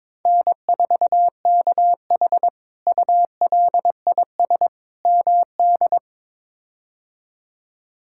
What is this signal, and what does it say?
Morse code, heard as a keyed tone, about 22 wpm, 700 Hz. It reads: N4KH ULIS MD
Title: K1USN SST QSO Practice - 22wpm